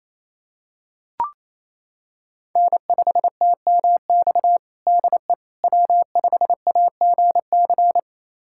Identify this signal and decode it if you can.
Morse code, heard as a keyed tone, about 27 wpm, 700 Hz. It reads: N5TMX DE W5AGC
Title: USA Callsign DE Callsign - 5 Characters - 28wpm